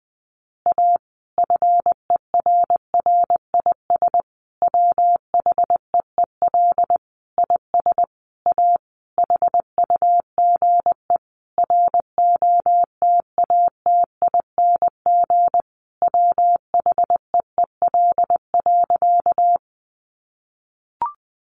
Morse code, heard as a keyed tone, about 20 wpm, 700 Hz. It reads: A FERRIS WHEEL IS A HUGE ROTATING WHEEL.
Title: Additional Easy Phrases 20wpm